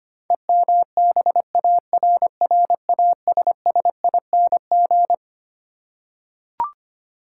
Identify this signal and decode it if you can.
Morse code, heard as a keyed tone, about 25 wpm, 700 Hz. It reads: EMBARRASSING
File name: Twelve Letter Words - 25wpm